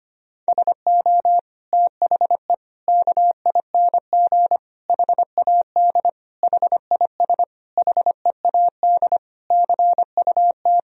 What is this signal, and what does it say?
Morse code, heard as a keyed tone, about 25 wpm, 700 Hz. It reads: SO THE KING HAD HIS HEAD CUT
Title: Sentences from Top 1000 Words - Review Part 1 - 25wpm